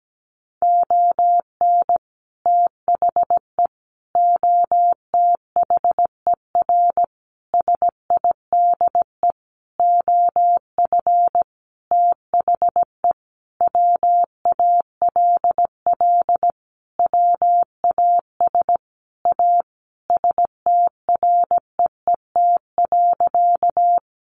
Morse code, heard as a keyed tone, about 17 wpm, 700 Hz. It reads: ON THE OTHER SIDE OF THE WALL WAS A STREET.
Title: Sentences from Top 1100 Words 17wpm